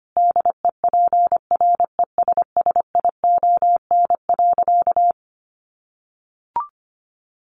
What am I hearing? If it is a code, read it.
Morse code, heard as a keyed tone, about 25 wpm, 700 Hz. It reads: DEPRESSION.
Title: Sentences from Top 1300 Words 25wpm